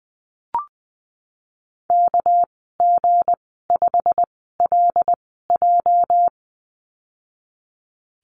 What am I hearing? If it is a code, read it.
Morse code, heard as a keyed tone, about 20 wpm, 700 Hz. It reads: K G 5 L J
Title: 207 Lesson 33g - Letter j - Call Signs 20wpm